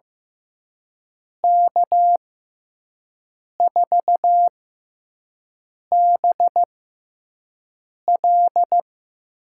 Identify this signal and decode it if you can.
Morse code, heard as a keyed tone, about 15 wpm, spaced as for 5 wpm, 700 Hz. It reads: K4BL
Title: USA Call Signs - 5 Characters - 6x Character Spacing - 15wpm